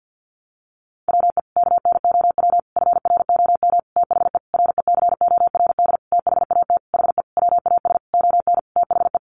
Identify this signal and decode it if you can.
Morse code, heard as a keyed tone, about 50 wpm, 700 Hz. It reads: WE KNOW FROM THE RECORD THAT HE WAS ON THE
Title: Sentences from Top 500 Words 50wpm